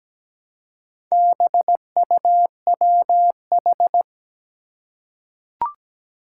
Morse code, heard as a keyed tone, about 17 wpm, 700 Hz. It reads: BUWH
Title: Four Challenging Letter-Number Combinations 17wpm